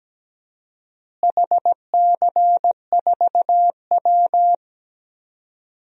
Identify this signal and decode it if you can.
Morse code, heard as a keyed tone, about 17 wpm, 700 Hz. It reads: HC4W
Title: DXCC-100 Call Signs - 4 Characters - Repeated 3x - 17wpm